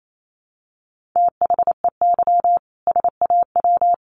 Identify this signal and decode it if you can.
Morse code, heard as a keyed tone, about 28 wpm, 700 Hz. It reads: THEY SAW
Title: A Sound Of Thunder 28wpm